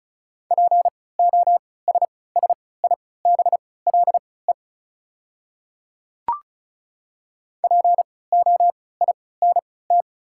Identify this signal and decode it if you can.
Morse code, heard as a keyed tone, about 36 wpm, spaced as for 20 wpm, 690 Hz. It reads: POSSIBLE POINT
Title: 3x Character Spacing - Top 500 Words - 35wpm